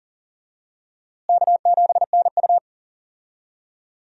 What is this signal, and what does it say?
Morse code, heard as a keyed tone, about 40 wpm, 700 Hz. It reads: K7NU
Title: USA Call Signs - 4 Characters - Repeated 3x - 40wpm